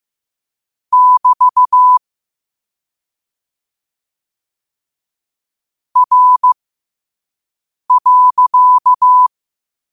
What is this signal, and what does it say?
Morse code, heard as a keyed tone, about 15 wpm, spaced as for 5 wpm, 1000 Hz. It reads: <BT> R.
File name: qotdfiles / 5wpm